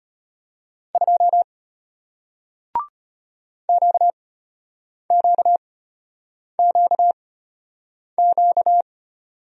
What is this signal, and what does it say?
Morse code, heard as a keyed tone, about 33 wpm, spaced as for 8 wpm, 690 Hz. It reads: 2 QQQQ